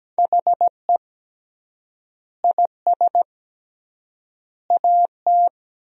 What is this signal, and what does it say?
Morse code, heard as a keyed tone, about 17 wpm, 700 Hz. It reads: HE IS AT
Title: 3x Word-Spacing - Sentences from Top 300 Words - 17wpm